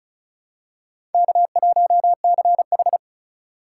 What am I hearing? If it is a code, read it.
Morse code, heard as a keyed tone, about 36 wpm, 700 Hz. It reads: K1CH
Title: USA Call Signs - 4 Characters - 2x Character Spacing - 35wpm